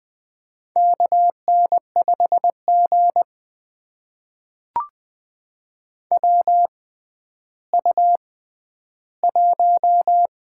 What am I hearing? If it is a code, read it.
Morse code, heard as a keyed tone, about 20 wpm, spaced as for 11 wpm, 700 Hz. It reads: KN5G WU1
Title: USA Call Signs - 4 Characters - 6x Character Spacing - 20wpm